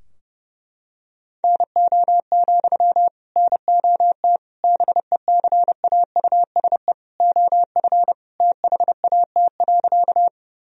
Morse code, heard as a keyed tone, about 32 wpm, 700 Hz. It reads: NO, NOT BECAUSE OF THAT.
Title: Sentences from Top 1000 Words - Review Part 3 - 30wpm